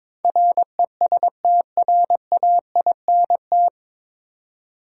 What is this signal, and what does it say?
Morse code, heard as a keyed tone, about 21 wpm, 700 Hz. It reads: RESTRAINT
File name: practice / 22wpm / Nine Letter Words - 22wpm